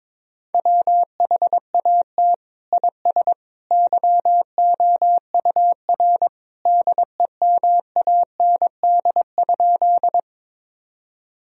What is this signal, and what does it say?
Morse code, heard as a keyed tone, about 22 wpm, 700 Hz. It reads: WHAT IS YOUR DEMAND?
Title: Sentences from Top 1000 Words - Review Part 4 - 22wpm